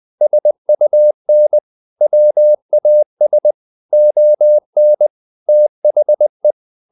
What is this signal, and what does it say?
Morse code, heard as a keyed tone, about 20 wpm, 600 Hz. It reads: SUN WAS ON THE
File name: the old man and the sea / the old man and the sea part 1 20wpm